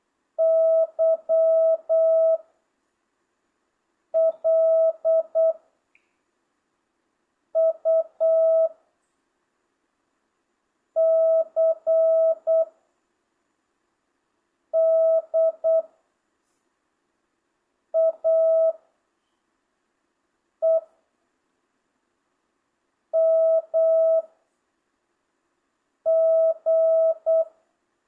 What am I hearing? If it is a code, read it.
Morse code, heard as a keyed tone, about 8 wpm, 660 Hz. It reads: Y L U C D A E M G